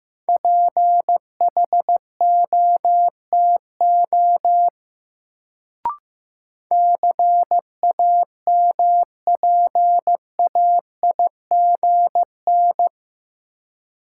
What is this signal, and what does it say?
Morse code, heard as a keyed tone, about 15 wpm, 700 Hz. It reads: PHOTO CAMPAIGN
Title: Sentences from Top 1200 Words - New Word Drill - 15wpm